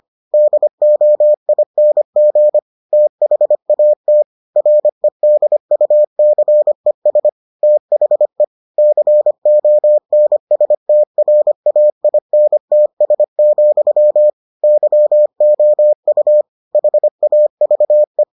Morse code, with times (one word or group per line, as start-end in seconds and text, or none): 0.3-2.6 DOING
2.9-4.2 THAT
4.6-7.3 REDUCES
7.6-8.4 THE
8.8-14.3 CONSTRAINTS,
14.6-16.4 YOU
16.8-18.2 HAVE